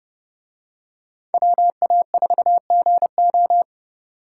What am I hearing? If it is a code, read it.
Morse code, heard as a keyed tone, about 30 wpm, 700 Hz. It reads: WA4GO